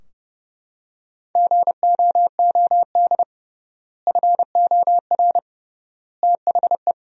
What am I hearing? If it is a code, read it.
Morse code, heard as a keyed tone, about 31 wpm, 700 Hz. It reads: GOOD FOR THE
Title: 3x Word-Spacing - Sentences from Top 300 Words - 30wpm